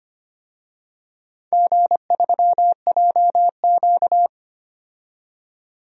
Morse code, heard as a keyed tone, about 25 wpm, 700 Hz. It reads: G3JQ
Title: DXCC-100 Call Signs - 4 Characters - 25wpm